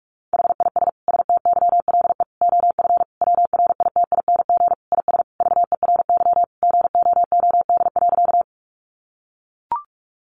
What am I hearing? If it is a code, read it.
Morse code, heard as a keyed tone, about 44 wpm, 700 Hz. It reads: HIS STYLE OF WRITING IS VERY GOOD.
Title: Sentences from Top 800 Words 45wpm